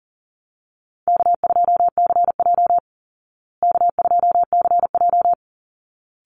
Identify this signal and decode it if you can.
Morse code, heard as a keyed tone, about 40 wpm, 700 Hz. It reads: K2CJ K2CJ
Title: Call Signs Repeated 3x - 40wpm